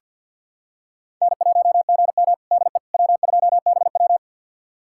Morse code, heard as a keyed tone, about 51 wpm, 700 Hz. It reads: N1GM DE W2BW
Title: USA Callsign DE Callsign - 4 Characters - 50wpm